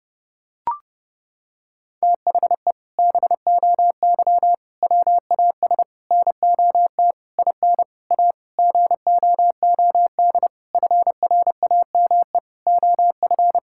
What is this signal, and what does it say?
Morse code, heard as a keyed tone, about 30 wpm, 700 Hz. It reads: THE BOY WAS NOT IN A GOOD FRAME OF
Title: Sentences from Top 1000 Words - Review Part 6 - 30wpm